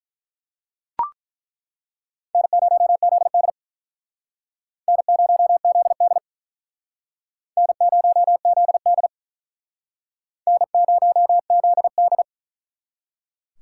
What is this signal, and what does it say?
Morse code, heard as a keyed tone, about 49 wpm, 700 Hz. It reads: N0ZD N0ZD N0ZD N0ZD